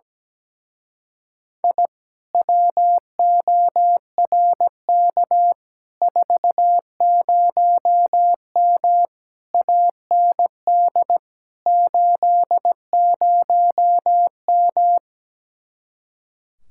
Morse code, heard as a keyed tone, about 17 wpm, 700 Hz. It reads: I WORK 40M AND 80M
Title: Speed-Racing Common QSO Phrases Encore - 17wpm